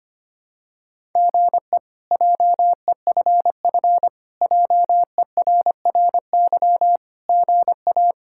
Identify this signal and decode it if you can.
Morse code, heard as a keyed tone, about 25 wpm, 700 Hz. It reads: GE JEFF JERRY GA
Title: K1USN SST QSO Practice - 25wpm